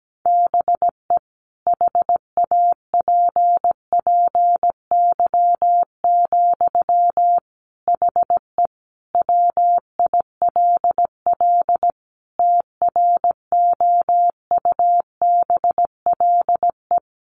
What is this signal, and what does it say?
Morse code, heard as a keyed tone, about 17 wpm, 700 Hz. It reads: BE HAPPY, HE WILL TROUBLE
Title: Sentences from Top 1200 Words 17wpm